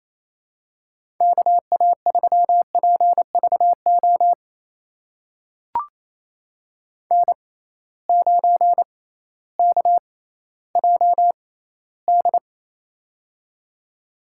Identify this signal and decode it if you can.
Morse code, heard as a keyed tone, about 28 wpm, spaced as for 15 wpm, 700 Hz. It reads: KA3PVO N9KJD
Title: USA Call Signs - 6x Character Spacing - 28wpm